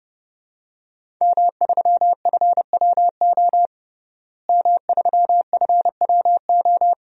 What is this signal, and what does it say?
Morse code, heard as a keyed tone, about 30 wpm, 700 Hz. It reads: M3FWO M3FWO
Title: DXCC-100 Call Signs - 5 Characters - Repeated 3x - 30wpm